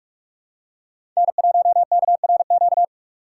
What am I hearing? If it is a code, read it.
Morse code, heard as a keyed tone, about 43 wpm, 700 Hz. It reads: N1KRQ
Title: USA Call Signs - 4x Character Spacing - 45wpm